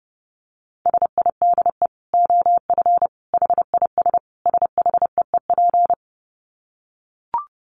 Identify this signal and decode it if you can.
Morse code, heard as a keyed tone, about 30 wpm, 700 Hz. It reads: SIDE OF HIS SHEEP
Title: Dit Challenge 30wpm